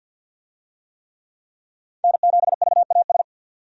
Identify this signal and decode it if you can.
Morse code, heard as a keyed tone, about 47 wpm, 700 Hz. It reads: N7VAS
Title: USA Call Signs - 5 Characters - 50wpm